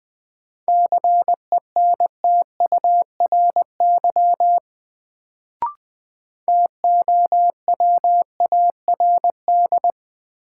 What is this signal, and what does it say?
Morse code, heard as a keyed tone, about 20 wpm, 700 Hz. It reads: CENTURY TOWARD
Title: Sets of 5 Words - New Word Drill - 20wpm